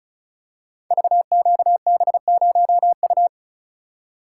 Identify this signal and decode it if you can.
Morse code, heard as a keyed tone, about 36 wpm, 700 Hz. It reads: VQB0U
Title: Code Groups - 5 Letter:Numbers - 35wpm